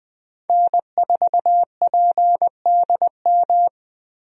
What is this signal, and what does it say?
Morse code, heard as a keyed tone, about 20 wpm, 700 Hz. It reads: N4PDM
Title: USA Call Signs - 5 Characters - Repeated 2x - 20wpm